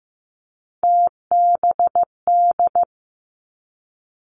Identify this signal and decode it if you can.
Morse code, heard as a keyed tone, about 15 wpm, 700 Hz. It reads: TBD
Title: Three Challenging Letter-Number Combinations 15wpm